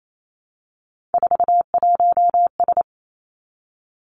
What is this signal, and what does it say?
Morse code, heard as a keyed tone, about 28 wpm, 700 Hz. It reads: 41S
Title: Three Challenging Letter-Number Combinations 28wpm